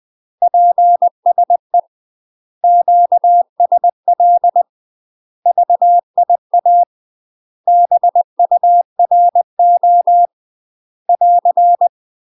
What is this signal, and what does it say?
Morse code, heard as a keyed tone, about 20 wpm, 700 Hz. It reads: PSE QSL VIA BURO <AR>
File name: UK MASTER QSOs TRAINING / 20WPM / UK MASTER QSOs 0046